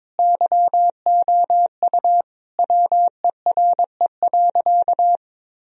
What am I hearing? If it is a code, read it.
Morse code, heard as a keyed tone, about 21 wpm, 700 Hz. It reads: YOU WERE.
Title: Sentences from Top 1000 Words - Review Part 4 - 22wpm